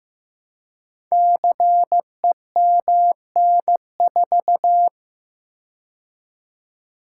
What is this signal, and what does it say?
Morse code, heard as a keyed tone, about 15 wpm, 700 Hz. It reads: CEMN4
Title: Code Groups - 5 Letter:Numbers - 15wpm